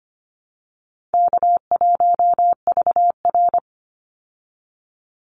Morse code, heard as a keyed tone, about 25 wpm, 700 Hz. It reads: K1VR
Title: Easy Call Signs Repeated 2x - 25wpm